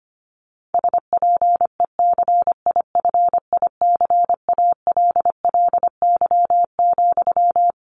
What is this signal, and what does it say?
Morse code, heard as a keyed tone, about 25 wpm, 700 Hz. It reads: SPECIFICALLY,
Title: Sentences from Top 1100 Words 25wpm